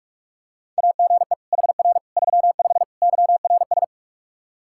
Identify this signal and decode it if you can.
Morse code, heard as a keyed tone, about 46 wpm, 700 Hz. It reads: AGE HR 35 YRS
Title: Practice QSO - Exchange 3 - More Info - 45wpm